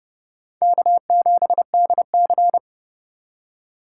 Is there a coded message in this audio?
Yes: Morse code from 0.6 to 2.6 s, heard as a keyed tone, about 29 wpm, 700 Hz. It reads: K7DC